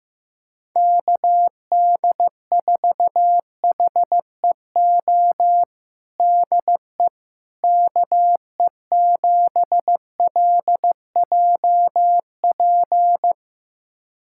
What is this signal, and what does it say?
Morse code, heard as a keyed tone, about 15 wpm, 700 Hz. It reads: KD4HEO DE KE7LJP
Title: USA Callsign DE Callsign 15wpm